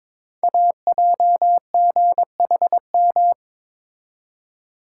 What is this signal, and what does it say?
Morse code, heard as a keyed tone, about 22 wpm, 700 Hz. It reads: AJGHM
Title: Code Groups - 5 Letters - 22wpm